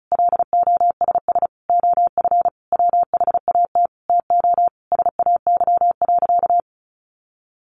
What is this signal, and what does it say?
Morse code, heard as a keyed tone, about 34 wpm, 700 Hz. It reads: LOSS OF WHAT TO SAY.
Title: Sentences from Top 1400 Words 35wpm